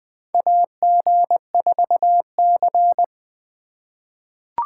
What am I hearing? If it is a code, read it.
Morse code, heard as a keyed tone, about 20 wpm, 700 Hz. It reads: AG4C
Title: USA Call Signs - 4 Characters - 3x Character Spacing - 20wpm